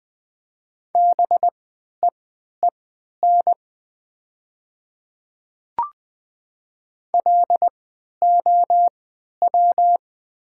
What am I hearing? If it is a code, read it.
Morse code, heard as a keyed tone, about 20 wpm, spaced as for 11 wpm, 700 Hz. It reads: BEEN LOW
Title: 3x Character Spacing - Top 200 Words - 20wpm